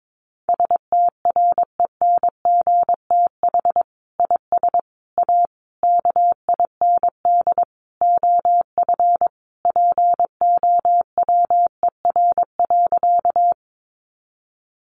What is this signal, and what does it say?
Morse code, heard as a keyed tone, about 22 wpm, 700 Hz. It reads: STRENGTH IS A KIND OF POWER.
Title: Sentences from Top 1300 Words 22wpm